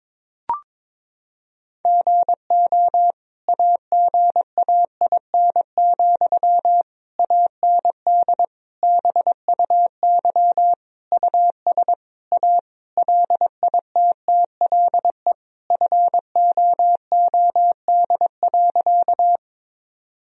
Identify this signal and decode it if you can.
Morse code, heard as a keyed tone, about 22 wpm, 700 Hz. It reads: GO AGAIN, AND BUY US A LITTLE FOOD.
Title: Sentences from Top 1000 Words - Review Part 4 - 22wpm